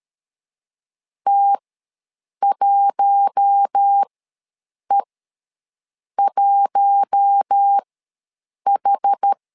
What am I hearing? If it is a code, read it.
Morse code, heard as a keyed tone, about 13 wpm, spaced as for 7 wpm, 800 Hz. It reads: T1E1H
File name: LEZIONI / 4 Q1 ws 12wpm 2129